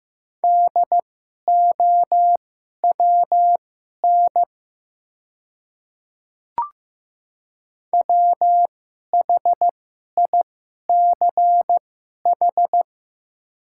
Morse code, heard as a keyed tone, about 15 wpm, spaced as for 11 wpm, 700 Hz. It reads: DOWN WHICH
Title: 2x Character Spacing - Top 100 Words - 15wpm